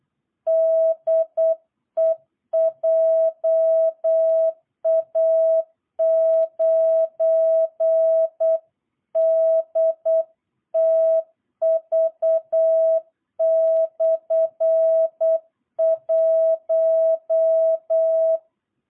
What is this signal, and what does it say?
Morse code, heard as a keyed tone, about 8 wpm, 660 Hz. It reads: DE JA9DTV/1
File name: jh8cbh / 2022cw